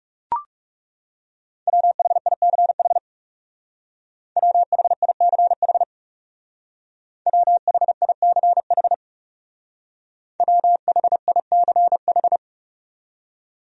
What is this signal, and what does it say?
Morse code, heard as a keyed tone, about 38 wpm, 700 Hz. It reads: WHICH WHICH WHICH WHICH